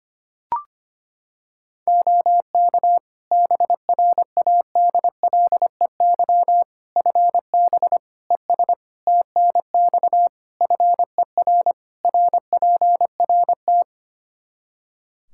Morse code, heard as a keyed tone, about 25 wpm, 700 Hz. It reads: OK BRADLEY FB ES TNX FER RPRT
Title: Practice QSO - Exchange 2 - Info - 25wpm